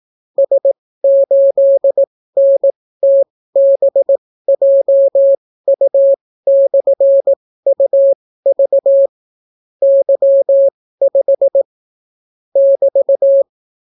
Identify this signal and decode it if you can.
Morse code, heard as a keyed tone, about 18 wpm, spaced as for 14 wpm, 550 Hz. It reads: S8NTBJU/UV Y5 <BT>